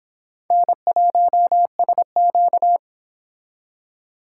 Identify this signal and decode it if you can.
Morse code, heard as a keyed tone, about 26 wpm, 700 Hz. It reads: N1SQ